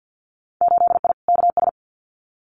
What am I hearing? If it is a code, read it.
Morse code, heard as a keyed tone, about 50 wpm, 700 Hz. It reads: 8I KS
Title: Winter Field Day Exchanges Encore 50wpm